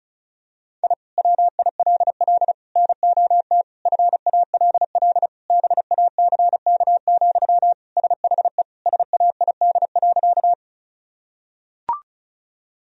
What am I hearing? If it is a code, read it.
Morse code, heard as a keyed tone, about 34 wpm, 700 Hz. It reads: I WILL NOT FALL BACK, SHE SAID.
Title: Sentences from Top 1000 Words - Review Part 2 - 35wpm